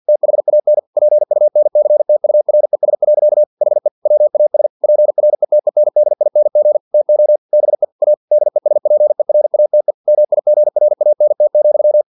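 Morse code, heard as a keyed tone, about 50 wpm, 600 Hz. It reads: THAN PUNCTURES. HE WAS PRETENDING TO BE A DESPERATE MIGRANT,
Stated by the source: Morse practice text from an English-language novel at 49 wpm.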